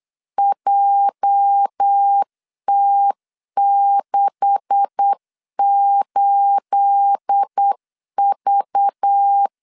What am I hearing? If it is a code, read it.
Morse code, heard as a keyed tone, about 8 wpm, 800 Hz. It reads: JT68V
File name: LEZIONI / 9 286 8wpm 3425